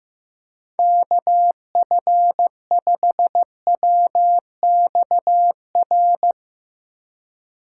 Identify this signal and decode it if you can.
Morse code, heard as a keyed tone, about 15 wpm, 700 Hz. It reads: KF5WXR
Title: USA Call Signs - 6 Characters - 2x Character Spacing - 15wpm